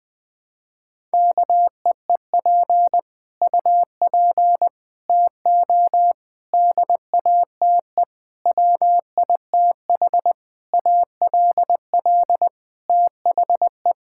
Morse code, heard as a keyed tone, about 20 wpm, 700 Hz. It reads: KEEP UP TO DATE WITH ALL THE
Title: Sets of 9 Words - 20wpm